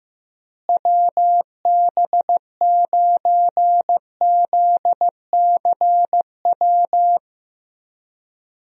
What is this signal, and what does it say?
Morse code, heard as a keyed tone, about 15 wpm, 700 Hz. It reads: WB9ZCW
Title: USA Call Signs - 6 Characters - Repeated 2x - 15wpm